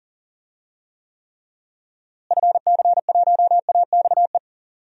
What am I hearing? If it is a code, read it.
Morse code, heard as a keyed tone, about 39 wpm, 700 Hz. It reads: FC1AXE
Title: DXCC-100 Call Signs - 6 Characters - Repeated 3x - 40wpm